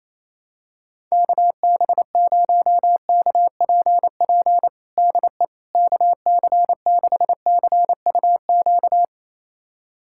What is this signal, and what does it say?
Morse code, heard as a keyed tone, about 28 wpm, 700 Hz. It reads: KB0KPP DE KC6CUQ
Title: USA Callsign DE Callsign 28wpm